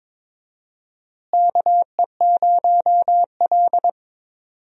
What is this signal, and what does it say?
Morse code, heard as a keyed tone, about 23 wpm, 700 Hz. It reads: KE0L
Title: USA Call Signs - 4 Characters - Repeated 3x - 22wpm